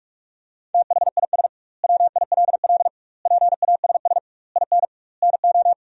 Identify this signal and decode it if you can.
Morse code, heard as a keyed tone, about 46 wpm, 700 Hz. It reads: THIS WILL PASS IN NO
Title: 2x Word-Spacing - Sentences from Top 400 Words - 45wpm